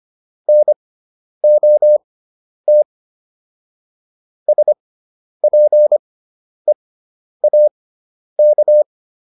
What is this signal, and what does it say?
Morse code, heard as a keyed tone, about 25 wpm, spaced as for 10 wpm, 600 Hz. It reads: NOT SPEAK